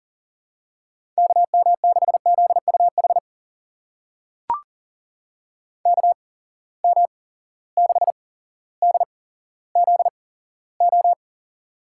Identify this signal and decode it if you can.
Morse code, heard as a keyed tone, about 40 wpm, spaced as for 17 wpm, 700 Hz. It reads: KM6ZUH KM6DZO
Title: USA Call Signs - 6 Characters - 8x Character Spacing - 40wpm